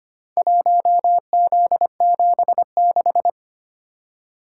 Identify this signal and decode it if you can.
Morse code, heard as a keyed tone, about 25 wpm, 700 Hz. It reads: 1Z76
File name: practice / 25wpm / Four Letter-Number Combinations - 25wpm